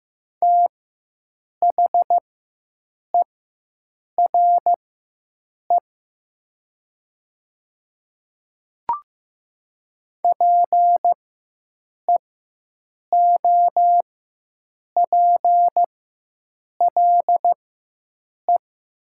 Morse code, heard as a keyed tone, about 15 wpm, spaced as for 7 wpm, 700 Hz. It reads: THERE PEOPLE
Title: 4x Character Spacing - Top 200 Words - 15wpm